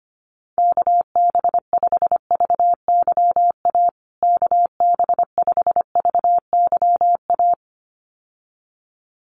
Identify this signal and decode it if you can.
Morse code, heard as a keyed tone, about 25 wpm, 700 Hz. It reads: KB5VYA KB5VYA
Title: USA Call Signs Repeated 2x 25wpm